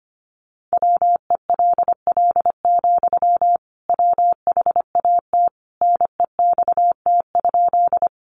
Morse code, heard as a keyed tone, about 25 wpm, 700 Hz. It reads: WELL, WHAT NEXT?
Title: Sentences from Top 300 Words 25wpm